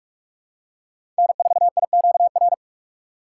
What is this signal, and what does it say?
Morse code, heard as a keyed tone, about 47 wpm, 700 Hz. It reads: N4IQR
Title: USA Call Signs - 5 Characters - 6x Character Spacing - 45wpm